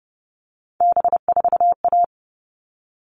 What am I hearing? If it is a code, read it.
Morse code, heard as a keyed tone, about 30 wpm, 700 Hz. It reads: B4A
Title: All Cut Numbers - 3-Digit 30wpm